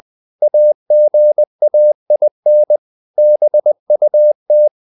Morse code, heard as a keyed tone, about 20 wpm, 600 Hz. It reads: AGAIN BUT
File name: the old man and the sea part 2 20wpm